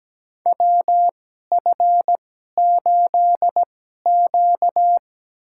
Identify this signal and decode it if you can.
Morse code, heard as a keyed tone, about 17 wpm, spaced as for 12 wpm, 700 Hz. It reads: WF8Q
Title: USA Call Signs - 4 Characters - 2x Character Spacing - 17wpm